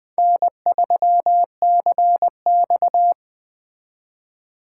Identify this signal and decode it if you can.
Morse code, heard as a keyed tone, about 20 wpm, 700 Hz. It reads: N3CX
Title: Speed Racing - USA 1x2 Call Signs 17wpm